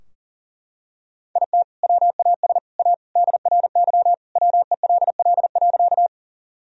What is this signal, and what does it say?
Morse code, heard as a keyed tone, about 37 wpm, 700 Hz. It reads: IT WAS A DRY WELL.
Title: Sentences from Top 1000 Words - Review Part 3 - 40wpm